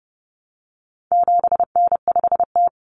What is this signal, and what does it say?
Morse code, heard as a keyed tone, about 30 wpm, 700 Hz. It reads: 7N5T
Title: Common Cut Numbers - 4-Digit 30wpm